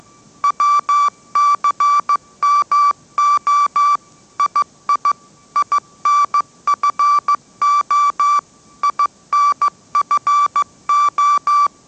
Morse code, heard as a keyed tone, about 15 wpm, 1160 Hz. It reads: WCMO II INFO INFO
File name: mrx / mrx066 16wpm